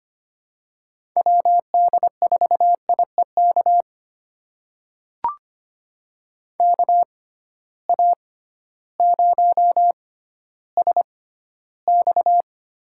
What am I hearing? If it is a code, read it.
Morse code, heard as a keyed tone, about 25 wpm, spaced as for 13 wpm, 700 Hz. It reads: WD4IEK KA0SX
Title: USA Call Signs - 6x Character Spacing - 25wpm